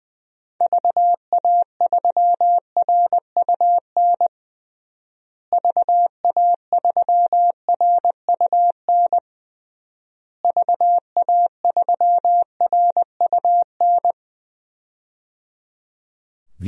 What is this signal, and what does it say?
Morse code, heard as a keyed tone, about 20 wpm, 700 Hz. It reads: VA3RUN VA3RUN VA3RUN
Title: DXCC-100 Call Signs - 6 Characters - Repeated 3x - 20wpm